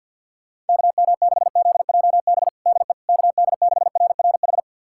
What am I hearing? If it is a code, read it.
Morse code, heard as a keyed tone, about 52 wpm, 700 Hz. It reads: KM6ZJB DE KD6RRH
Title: USA Callsign DE Callsign 50wpm